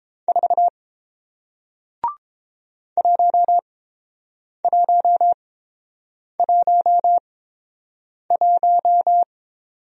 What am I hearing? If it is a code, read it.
Morse code, heard as a keyed tone, about 30 wpm, spaced as for 7 wpm, 700 Hz. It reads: 4 1111